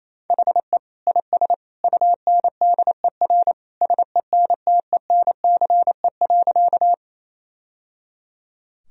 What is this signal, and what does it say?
Morse code, heard as a keyed tone, about 28 wpm, 700 Hz. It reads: HE IS UNDER SENTENCE.
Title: Sentences from Top 1000 Words - Review Part 1 - 28wpm